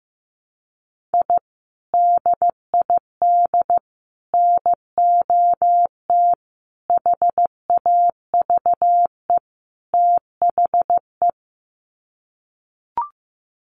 Morse code, heard as a keyed tone, about 14 wpm, 700 Hz. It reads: I DID NOT HAVE THE
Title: Sets of 5 Words Encore 15wpm